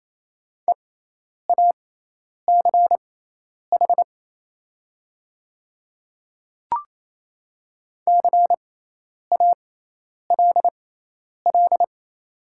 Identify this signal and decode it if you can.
Morse code, heard as a keyed tone, about 28 wpm, spaced as for 10 wpm, 700 Hz. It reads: EACH CALL
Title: 6x Character Spacing - Top 100 Words - 28wpm